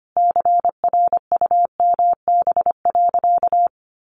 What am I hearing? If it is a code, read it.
Morse code, heard as a keyed tone, about 25 wpm, 700 Hz. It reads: CRUMB.
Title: Additional Easy Phrases 25wpm